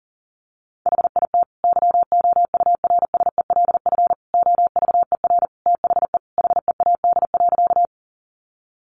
Morse code, heard as a keyed tone, about 40 wpm, 700 Hz. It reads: HIT YOURSELF OVER THE HEAD.
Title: Sentences from Top 1200 Words 40wpm